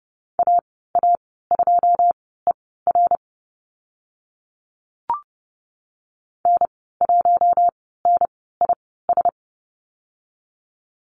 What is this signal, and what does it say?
Morse code, heard as a keyed tone, about 30 wpm, spaced as for 17 wpm, 700 Hz. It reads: AA2ER N1NIS
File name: Lesson 14g - Number 2 - Call Signs 30wpm